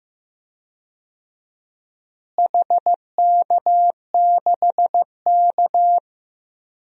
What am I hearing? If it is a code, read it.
Morse code, heard as a keyed tone, about 16 wpm, 700 Hz. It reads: HK6K
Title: DXCC-100 Call Signs - 4 Characters - Repeated 2x - 15wpm